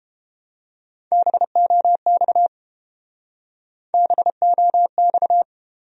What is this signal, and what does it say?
Morse code, heard as a keyed tone, about 31 wpm, 700 Hz. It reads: BOX BOX